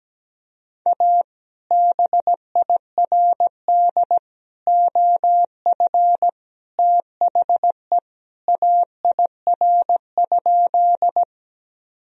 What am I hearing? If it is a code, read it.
Morse code, heard as a keyed tone, about 17 wpm, 700 Hz. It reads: A BIRD OF THE AIR?
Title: Sentences from Top 1000 Words - Review Part 2 - 17wpm